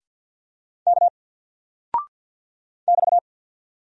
Morse code, heard as a keyed tone, about 50 wpm, 700 Hz. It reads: K <BT>
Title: Single Character (Letter:Number:Punctuation:Prosign) 50wpm